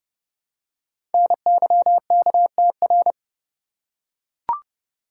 Morse code, heard as a keyed tone, about 30 wpm, 690 Hz. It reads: NYKTR
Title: Code Groups - 5 Letters - 30wpm